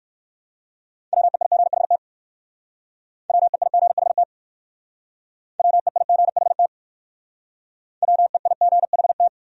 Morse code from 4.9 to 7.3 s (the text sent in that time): WEIGHT